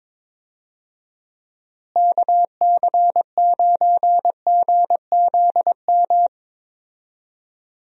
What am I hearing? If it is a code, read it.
Morse code, heard as a keyed tone, about 21 wpm, 700 Hz. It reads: KC9GZM
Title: USA Call Signs - 6 Characters - Repeated 3x - 22wpm